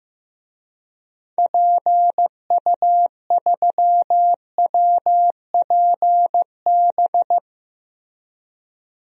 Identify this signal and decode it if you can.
Morse code, heard as a keyed tone, about 15 wpm, 700 Hz. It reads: PU3WPB
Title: DXCC-100 Call Signs - 6 Characters - Repeated 3x - 15wpm